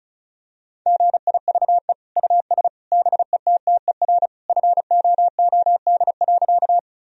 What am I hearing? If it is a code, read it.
Morse code, heard as a keyed tone, about 34 wpm, 700 Hz. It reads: GIVE US BETTER FOOD.
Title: Sentences from Top 1000 Words - Review Part 1 - 35wpm